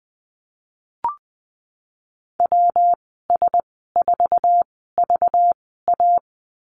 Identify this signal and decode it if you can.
Morse code, heard as a keyed tone, about 20 wpm, spaced as for 15 wpm, 700 Hz. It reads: WS4VA
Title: 193 Lesson 31g - Letter v - Call Signs 20wpm